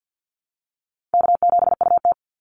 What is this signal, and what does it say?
Morse code, heard as a keyed tone, about 50 wpm, 700 Hz. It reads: K7UT
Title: Easy Call Signs Repeated 2x - 50wpm